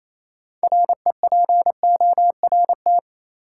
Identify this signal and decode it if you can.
Morse code, heard as a keyed tone, about 28 wpm, 700 Hz. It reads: REPORT
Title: Sentences from Top 900 Words - New Word Drill - 28wpm